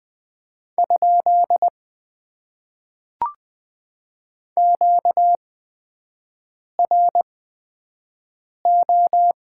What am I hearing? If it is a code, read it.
Morse code, heard as a keyed tone, about 20 wpm, spaced as for 5 wpm, 700 Hz. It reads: ? QRO
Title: Top 90 QSO Elements - 8x Character Spacing - 20wpm